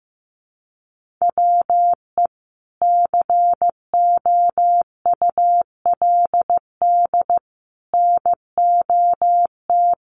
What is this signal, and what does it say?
Morse code, heard as a keyed tone, about 15 wpm, 700 Hz. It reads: WE COULD NOT
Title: Sentences from Top 1500 Words 15wpm